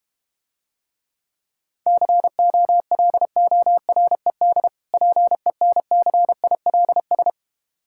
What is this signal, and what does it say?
Morse code, heard as a keyed tone, about 32 wpm, 700 Hz. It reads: COLORED PENCILS